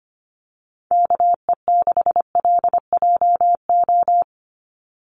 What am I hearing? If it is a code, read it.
Morse code, heard as a keyed tone, about 25 wpm, 700 Hz. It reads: KE6LJO
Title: USA Call Signs 25wpm